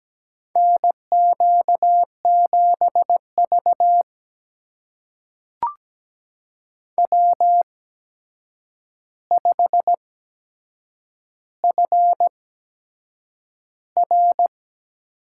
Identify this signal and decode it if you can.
Morse code, heard as a keyed tone, about 17 wpm, spaced as for 7 wpm, 700 Hz. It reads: NQ7V W5FR
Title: USA Call Signs - 4 Characters - 8x Character Spacing - 17wpm